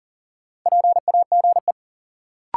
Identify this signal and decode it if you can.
Morse code, heard as a keyed tone, about 40 wpm, 700 Hz. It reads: PAGE
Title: Sentences from Top 1000 Words - Review Part 1 - New Word Drill - 40wpm